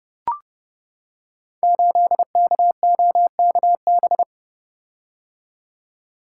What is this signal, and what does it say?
Morse code, heard as a keyed tone, about 30 wpm, 700 Hz. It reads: 8KOKB
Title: Code Groups - 5 Letter:Numbers - 30wpm